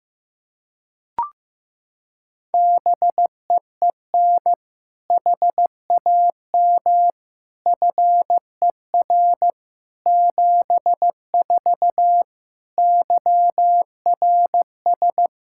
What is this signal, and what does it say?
Morse code, heard as a keyed tone, about 16 wpm, 700 Hz. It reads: BEEN HAM FER 74 YRS
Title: Practice QSO - Exchange 3 - More Info - 15wpm